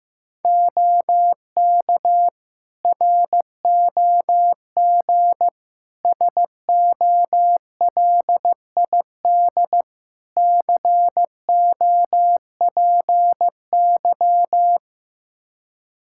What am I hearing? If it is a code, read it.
Morse code, heard as a keyed tone, about 15 wpm, 700 Hz. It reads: OK ROG SOLID COPY
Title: Practice QSO - Exchange 3 - More Info - 15wpm